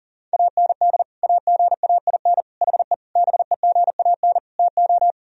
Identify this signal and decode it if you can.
Morse code, heard as a keyed tone, about 40 wpm, 700 Hz. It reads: AND AGAIN HE BEGAN TO
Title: Sentences from Top 1000 Words - Review Part 2 - 40wpm